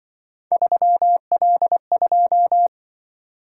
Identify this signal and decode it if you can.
Morse code, heard as a keyed tone, about 25 wpm, 700 Hz. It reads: 3L2